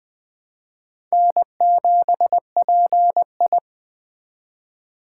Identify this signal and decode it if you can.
Morse code, heard as a keyed tone, about 20 wpm, 700 Hz. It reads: N7PI
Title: Speed Racing - USA 1x2 Call Signs 17wpm